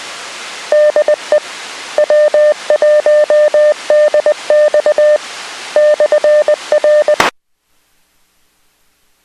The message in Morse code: DE W1DX /R